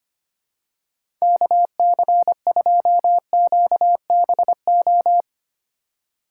Morse code, heard as a keyed tone, about 25 wpm, 700 Hz. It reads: KC2QBO
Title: USA Call Signs - 2x Character Spacing - 25wpm